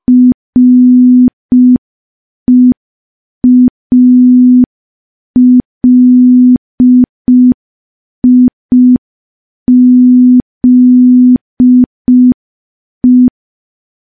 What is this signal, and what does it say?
Morse code, heard as a keyed tone, about 5 wpm, 250 Hz. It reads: REALIZE
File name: morse / morse-training-05WPM